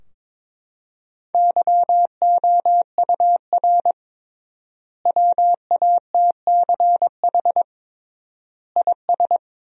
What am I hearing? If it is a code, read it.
Morse code, heard as a keyed tone, about 22 wpm, 700 Hz. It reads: YOUR WATCH IS
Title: 3x Word-Spacing - Sentences from Top 400 Words - 22wpm